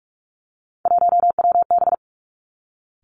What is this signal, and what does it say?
Morse code, heard as a keyed tone, about 46 wpm, 700 Hz. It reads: 1WB
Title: Three Challenging Letter-Number Combinations 45wpm